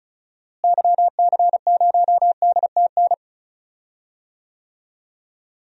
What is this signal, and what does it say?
Morse code, heard as a keyed tone, about 34 wpm, 700 Hz. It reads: YC0DTN
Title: DXCC-100 Call Signs - 6 Characters - Repeated 2x - 35wpm